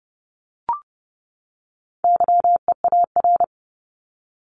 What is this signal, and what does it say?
Morse code, heard as a keyed tone, about 30 wpm, 700 Hz. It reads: YEAR
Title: ES:S Suffix - Top 100 Words with Base Words 30wpm